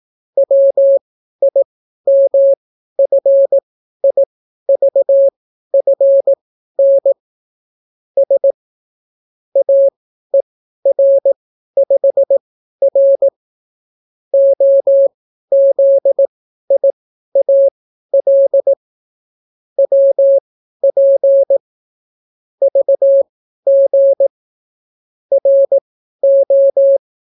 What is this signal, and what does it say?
Morse code, heard as a keyed tone, about 18 wpm, spaced as for 12 wpm, 550 Hz. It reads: WIMFIVFN S AER5R OZIAL WP VG RO